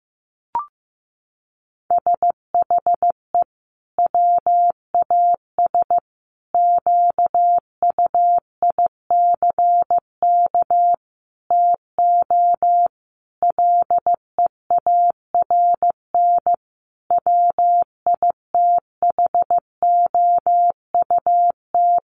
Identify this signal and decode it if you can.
Morse code, heard as a keyed tone, about 15 wpm, 700 Hz. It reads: SHE WAS QUICK TO LEARN WITHOUT
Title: Sentences from Top 1200 Words 15wpm